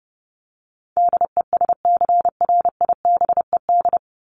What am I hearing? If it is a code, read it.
Morse code, heard as a keyed tone, about 30 wpm, 700 Hz. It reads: DESCRIBED
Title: ED Suffix - Top 100 Words with Base Words 30wpm